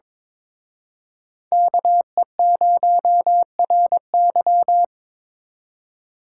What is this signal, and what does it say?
Morse code, heard as a keyed tone, about 26 wpm, 700 Hz. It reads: KE0RY
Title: Speed Racing - USA Call Signs 22wpm